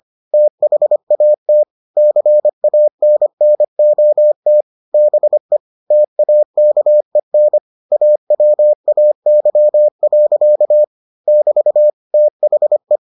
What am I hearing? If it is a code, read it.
Morse code, heard as a keyed tone, about 25 wpm, 600 Hz. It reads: THAT CANNOT BE TAKEN AWAY. <BT> THE